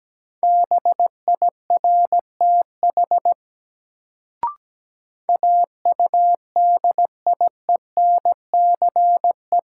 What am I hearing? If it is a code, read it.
Morse code, heard as a keyed tone, about 17 wpm, 700 Hz. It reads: BIRTH AUDIENCE
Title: Sentences from Top 800 Words - New Word Drill - 17wpm